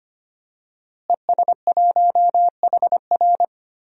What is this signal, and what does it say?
Morse code, heard as a keyed tone, about 25 wpm, 700 Hz. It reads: ES1HR
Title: DXCC-100 Call Signs - 5 Characters - Repeated 3x - 25wpm